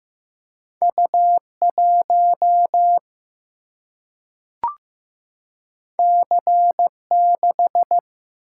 Morse code, heard as a keyed tone, about 15 wpm, 700 Hz. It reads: U1 C6
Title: Single-Letter Single-Number 15wpm